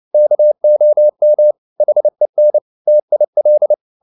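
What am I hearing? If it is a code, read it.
Morse code, heard as a keyed tone, about 29 wpm, 600 Hz. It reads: KOM HEN TIL
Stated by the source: Fast (QRQ) Morse at 29 wpm, from CW med Gnister.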